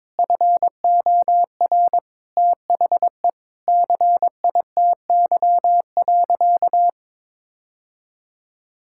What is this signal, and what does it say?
Morse code, heard as a keyed tone, about 21 wpm, 700 Hz. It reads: FOR THE CITY.
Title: Sentences from Top 1600 Words 22wpm